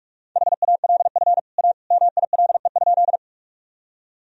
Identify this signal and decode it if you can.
Morse code, heard as a keyed tone, about 42 wpm, 700 Hz. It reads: HALF A MILE?
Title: Sentences from Top 1000 Words - Review Part 1 - 45wpm